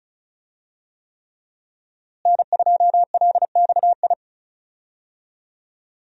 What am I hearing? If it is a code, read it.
Morse code, heard as a keyed tone, about 36 wpm, 700 Hz. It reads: N2LXI
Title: USA Call Signs - 5 Characters - Repeated 3x - 35wpm